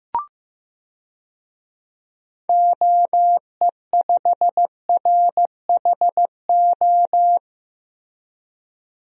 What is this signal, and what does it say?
Morse code, heard as a keyed tone, about 15 wpm, 700 Hz. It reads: OE5RHO
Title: DXCC-100 Call Signs - 6 Characters - Repeated 3x - 15wpm